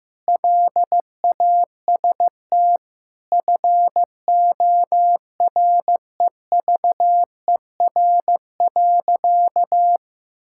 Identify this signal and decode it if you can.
Morse code, heard as a keyed tone, about 15 wpm, 700 Hz. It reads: LAST FOREVER.
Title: Sentences from Top 1600 Words 15wpm